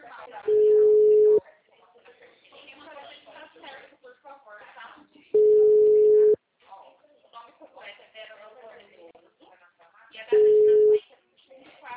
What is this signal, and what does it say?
Ringback tone (Europe, 425 Hz), with speech.